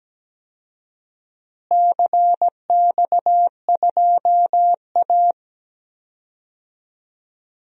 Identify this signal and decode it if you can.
Morse code, heard as a keyed tone, about 17 wpm, 700 Hz. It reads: CX2A
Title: DXCC-100 Call Signs - 4 Characters - Repeated 3x - 17wpm